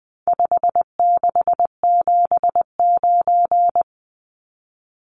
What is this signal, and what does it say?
Morse code, heard as a keyed tone, about 20 wpm, 700 Hz. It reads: 5679
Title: Numbers - 4-Digit 20wpm